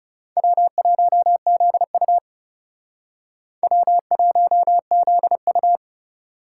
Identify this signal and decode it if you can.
Morse code, heard as a keyed tone, about 38 wpm, 700 Hz. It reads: W1ZU W1ZU